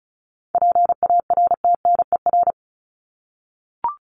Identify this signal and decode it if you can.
Morse code, heard as a keyed tone, about 34 wpm, 700 Hz. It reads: PARTNER
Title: Top 1501-1750 Words 35wpm